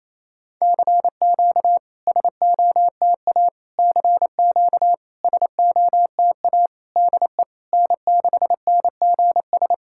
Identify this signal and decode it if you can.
Morse code, heard as a keyed tone, about 28 wpm, 700 Hz. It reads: CQ SOTA CQ SOTA DE N6NGS
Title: SOTA QSO Practice 28wpm